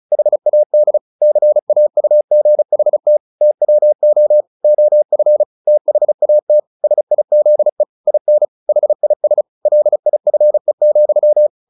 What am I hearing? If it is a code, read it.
Morse code, heard as a keyed tone, about 34 wpm, 600 Hz. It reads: HAD CAUGHT TWO OF THAT SIZE IN HIS LIFE,
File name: the old man and the sea / the old man and the sea 1 35wpm